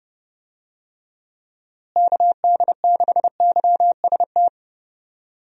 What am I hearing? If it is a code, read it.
Morse code, heard as a keyed tone, about 30 wpm, 700 Hz. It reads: KD6YST
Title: USA Call Signs - 6 Characters - Repeated 3x - 30wpm